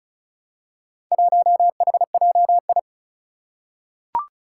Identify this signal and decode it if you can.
Morse code, heard as a keyed tone, about 36 wpm, 700 Hz. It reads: 1HJI
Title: Four Challenging Letter-Number Combinations 35wpm